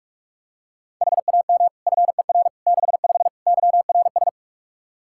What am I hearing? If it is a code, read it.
Morse code, heard as a keyed tone, about 46 wpm, 700 Hz. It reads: HAM FER 65 YRS
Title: Practice QSO - Exchange 3 - More Info - 45wpm